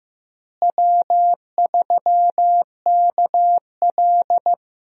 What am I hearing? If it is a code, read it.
Morse code, heard as a keyed tone, about 15 wpm, 700 Hz. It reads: W3KL
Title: USA Call Signs - 4 Characters - Repeated 3x - 15wpm